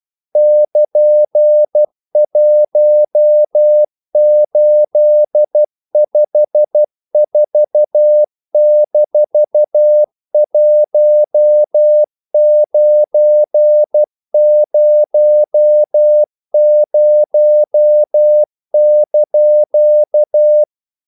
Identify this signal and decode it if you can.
Morse code, heard as a keyed tone, about 12 wpm, 600 Hz. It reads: <KN>1854-1900)
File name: CW / Practice / EnglishQuotations0312